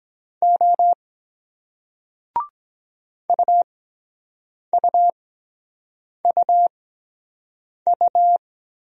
Morse code, heard as a keyed tone, about 23 wpm, spaced as for 7 wpm, 700 Hz. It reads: O UUUU